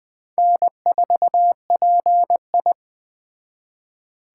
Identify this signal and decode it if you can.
Morse code, heard as a keyed tone, about 21 wpm, 700 Hz. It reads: N4PI
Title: Speed Racing - USA 1x2 Call Signs 17wpm